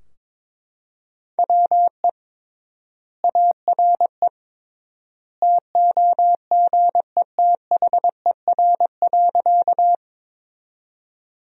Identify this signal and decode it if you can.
Morse code, heard as a keyed tone, about 22 wpm, 700 Hz. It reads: WE ARE TOGETHER.
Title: 3x Word-Spacing - Sentences from Top 300 Words - 22wpm